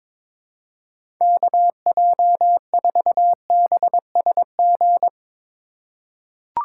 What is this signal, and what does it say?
Morse code, heard as a keyed tone, about 23 wpm, 700 Hz. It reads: KJ4BSG
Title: USA Call Signs - 4x Character Spacing - 22wpm